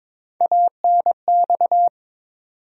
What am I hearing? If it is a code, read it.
Morse code, heard as a keyed tone, about 21 wpm, 700 Hz. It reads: ANX
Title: USA Call Sign Postfixes - 3 Letters 22wpm